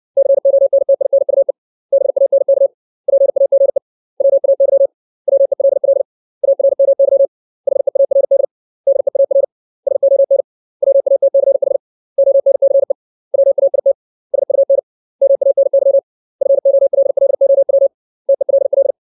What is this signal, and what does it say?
Morse code, heard as a keyed tone, about 60 wpm, 550 Hz. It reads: CONTINUE BANK PAGE WAY WELL ARMY HEARD DEAR SON WATCH ONCE WAIT SUN MANY FOLLOW TELL
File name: 632 words / 632 Words 60WPM